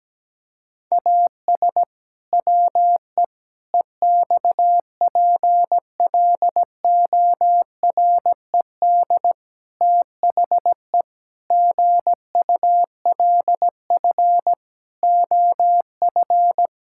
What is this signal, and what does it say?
Morse code, heard as a keyed tone, about 17 wpm, 700 Hz. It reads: AS WE EXPLORED THE GULF OF
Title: Pangram Challenge v2 - 17wpm